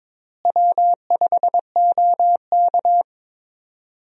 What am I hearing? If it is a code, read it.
Morse code, heard as a keyed tone, about 22 wpm, 700 Hz. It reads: W5OK
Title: USA Call Signs - 4 Characters - Repeated 2x - 22wpm